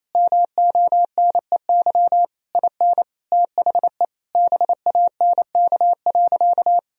Morse code, heard as a keyed tone, about 28 wpm, 700 Hz. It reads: MONEY IN THE BANK.
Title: Sentences from Top 1000 Words - Review Part 4 - 28wpm